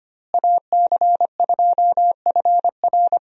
Morse code, heard as a keyed tone, about 25 wpm, 700 Hz. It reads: AC2FR
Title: USA Call Signs - 5 Characters - Repeated 3x - 25wpm